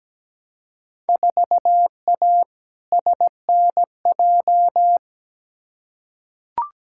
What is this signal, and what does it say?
Morse code, heard as a keyed tone, about 17 wpm, 700 Hz. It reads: 4A SNJ
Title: Field Day Exchanges Encore - 17wpm